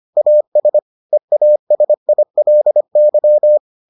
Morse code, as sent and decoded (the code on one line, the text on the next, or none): .- ... / . .- ... .. .-.. -.--
AS EASILY